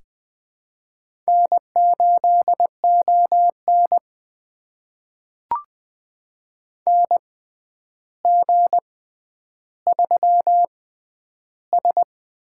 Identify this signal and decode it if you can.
Morse code, heard as a keyed tone, about 20 wpm, spaced as for 10 wpm, 700 Hz. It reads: N8ON NG3S
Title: USA Call Signs - 6x Character Spacing - 20wpm